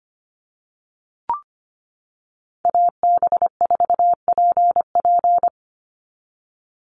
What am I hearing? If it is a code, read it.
Morse code, heard as a keyed tone, about 25 wpm, 700 Hz. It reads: AB4PP
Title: NA SOTA Chaser Callsigns 25wpm